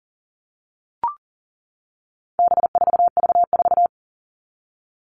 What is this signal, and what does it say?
Morse code, heard as a keyed tone, about 40 wpm, 700 Hz. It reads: B4V4
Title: All Cut Numbers - 4-Digit 40wpm